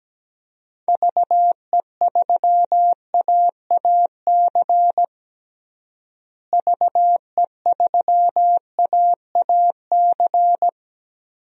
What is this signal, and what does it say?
Morse code, heard as a keyed tone, about 17 wpm, 700 Hz. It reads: VE3AAC VE3AAC
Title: DXCC-100 Call Signs - 6 Characters - Repeated 3x - 17wpm